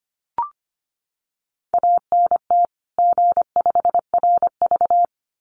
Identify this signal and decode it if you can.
Morse code, heard as a keyed tone, about 25 wpm, 700 Hz. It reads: ANT G5RV
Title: Common QSO Phrases 25wpm